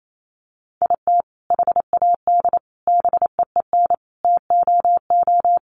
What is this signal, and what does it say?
Morse code, heard as a keyed tone, about 28 wpm, 700 Hz. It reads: IT HAD BEEN TOO
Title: Sentences from Top 1300 Words 28wpm